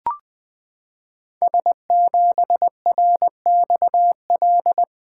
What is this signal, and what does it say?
Morse code, heard as a keyed tone, about 20 wpm, 700 Hz. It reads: S7RXL
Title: Code Groups - 5 Letter:Numbers - 20wpm